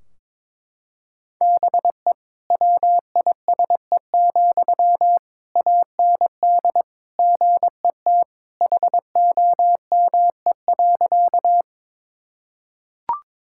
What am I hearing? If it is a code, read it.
Morse code, heard as a keyed tone, about 22 wpm, 700 Hz. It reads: BE WISE, AND GET HOME.
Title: Sentences from Top 1600 Words 22wpm